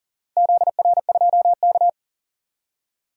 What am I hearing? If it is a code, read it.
Morse code, heard as a keyed tone, about 39 wpm, 700 Hz. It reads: ZR2K
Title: DXCC-100 Call Signs - 4 Characters - 40wpm